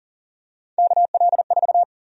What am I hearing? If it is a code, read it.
Morse code, heard as a keyed tone, about 40 wpm, 700 Hz. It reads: KL4
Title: USA Call Sign Prefixes - 40wpm